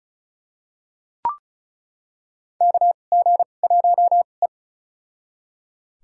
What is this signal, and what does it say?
Morse code, heard as a keyed tone, about 34 wpm, spaced as for 25 wpm, 700 Hz. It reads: KG1E
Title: USA Call Signs - 4 Characters - 2x Character Spacing - 35wpm